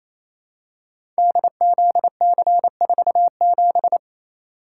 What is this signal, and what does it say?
Morse code, heard as a keyed tone, about 28 wpm, 700 Hz. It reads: DZC47
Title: Code Groups - 5 Letter:Numbers - 28wpm